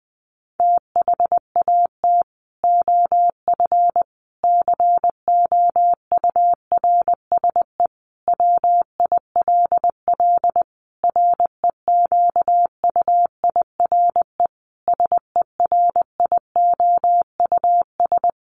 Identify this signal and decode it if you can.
Morse code, heard as a keyed tone, about 20 wpm, 700 Hz. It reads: THAT OF COURSE WILL REQUIRE SERIOUS
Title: Sentences from Top 1500 Words 20wpm